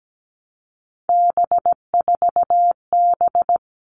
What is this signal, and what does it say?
Morse code, heard as a keyed tone, about 17 wpm, 700 Hz. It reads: B4B
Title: All Cut Numbers - 3-Digit 17wpm